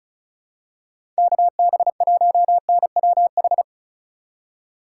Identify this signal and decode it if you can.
Morse code, heard as a keyed tone, about 34 wpm, 700 Hz. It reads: KB1NWH
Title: USA Call Signs - 2x Character Spacing - 35wpm